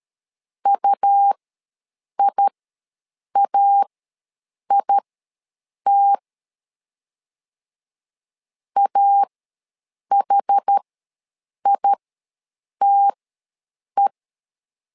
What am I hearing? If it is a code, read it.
Morse code, heard as a keyed tone, about 13 wpm, spaced as for 7 wpm, 800 Hz. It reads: UIAIT AHITE
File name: LEZIONI / 2 UHDN / 2 UHD ws 12wpm 1722